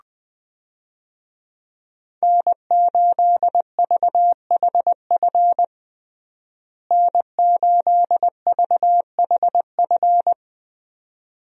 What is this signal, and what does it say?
Morse code, heard as a keyed tone, about 21 wpm, 700 Hz. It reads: N8VHF N8VHF
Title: USA Call Signs - 5 Characters - Repeated 2x - 20wpm